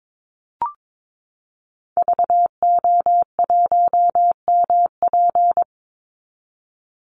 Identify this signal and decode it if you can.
Morse code, heard as a keyed tone, about 22 wpm, 700 Hz. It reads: VO1MP
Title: Call Signs Repeated 3x - 22wpm